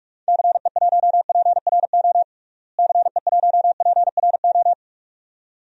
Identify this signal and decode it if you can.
Morse code, heard as a keyed tone, about 43 wpm, 700 Hz. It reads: CE1PRO CE1PRO
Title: DXCC-100 Call Signs - 6 Characters - Repeated 2x - 45wpm